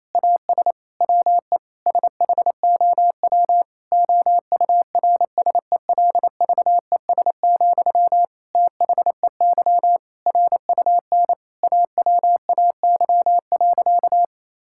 Morse code, heard as a keyed tone, about 28 wpm, 700 Hz. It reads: AS WE SHOW OURSELVES, THEY RUN AWAY.
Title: Sentences from Top 1600 Words 28wpm